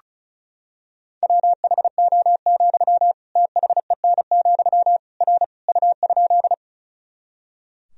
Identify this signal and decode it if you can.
Morse code, heard as a keyed tone, about 36 wpm, 700 Hz. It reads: WHO, THEN, R U?
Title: Sentences from Top 100 Words and Abbreviations 35wpm